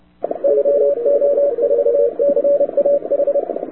Morse code, heard as a keyed tone, about 30 wpm, 570 Hz, with radio noise. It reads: YO4CAH
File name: wwcw08 / k5zd wwcw08 1130z